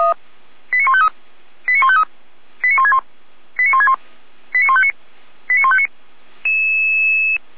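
Five-tone selective call (ZVEI), heard as quick runs of five tones, tones of about 70 ms. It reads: 89153 89153 89161 89161 89159 89159
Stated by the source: ZVEI five-tone selective call 89099 received on an SDR, from the Feuerwehr Orlamünde audio archive.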